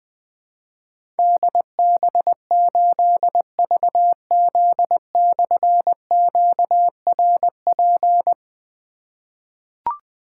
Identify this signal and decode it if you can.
Morse code, heard as a keyed tone, about 20 wpm, 700 Hz. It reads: DB8VZ/QRP
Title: Difficult Call Signs 20wpm